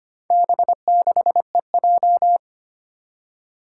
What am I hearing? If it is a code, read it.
Morse code, heard as a keyed tone, about 25 wpm, 700 Hz. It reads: B6EJ
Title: Four Challenging Letter-Number Combinations 25wpm